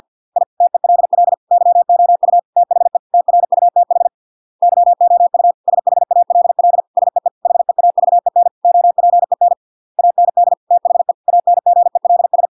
Morse code, heard as a keyed tone, about 50 wpm, 700 Hz. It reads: I TELL YOU THE TRUTH YOU SHALL SEE HEAVEN OPEN AND THE ANGELS